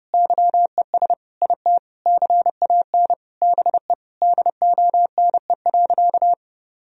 Morse code, heard as a keyed tone, about 30 wpm, 700 Hz. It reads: YES IT CAN BE DONE.
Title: Sentences from Top 1000 Words - Review Part 1 - 30wpm